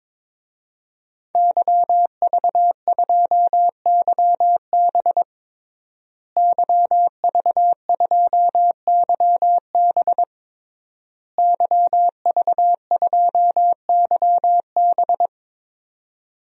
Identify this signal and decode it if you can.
Morse code, heard as a keyed tone, about 22 wpm, 700 Hz. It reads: YV2YB YV2YB YV2YB
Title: DXCC-100 Call Signs - 5 Characters - Repeated 3x - 22wpm